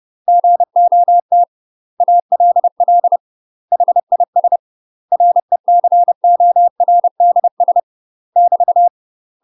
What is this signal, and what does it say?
Morse code, heard as a keyed tone, about 30 wpm, 700 Hz. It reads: GOT ALL HIS RECORDS <BT>
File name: UK MASTER QSOs TRAINING / 30WPM / UK MASTER QSOs 0015